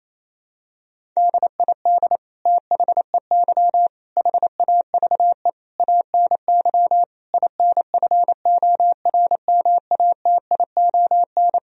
Morse code, heard as a keyed tone, about 28 wpm, 700 Hz. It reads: DID THEY HAVE ANY INFORMATION
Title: Sentences from Top 1000 Words - Review Part 3 - 28wpm